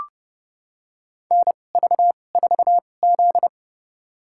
Morse code, heard as a keyed tone, about 29 wpm, spaced as for 21 wpm, 700 Hz. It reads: NV4Z
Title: USA Call Signs - 4 Characters - 2x Character Spacing - 30wpm